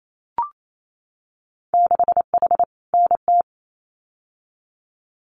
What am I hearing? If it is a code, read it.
Morse code, heard as a keyed tone, about 28 wpm, 700 Hz. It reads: 6H NT
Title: Winter Field Day Exchanges Encore 28wpm